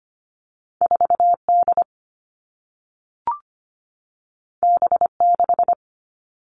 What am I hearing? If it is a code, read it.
Morse code, heard as a keyed tone, about 25 wpm, 700 Hz. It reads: 4D B6
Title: All Cut Numbers - 2-Digit 25wpm